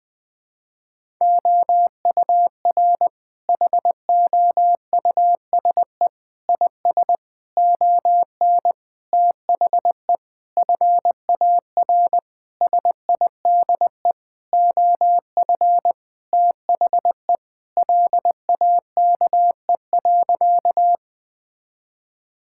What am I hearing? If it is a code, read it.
Morse code, heard as a keyed tone, about 20 wpm, 700 Hz. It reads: OUR HOUSE IS ON THE FAR SIDE OF THE LAKE.
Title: Sentences with 5-Word Phrases - Spoken Beginning - 20wpm